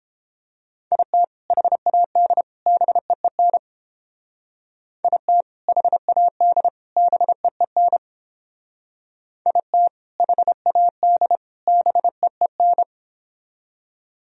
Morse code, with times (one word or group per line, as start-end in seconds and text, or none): 0.9-1.2 IT
1.5-2.4 HAD
2.7-3.6 BEEN
5.0-5.4 IT
5.7-6.7 HAD
7.0-8.0 BEEN
9.5-9.9 IT
10.2-11.4 HAD
11.7-12.8 BEEN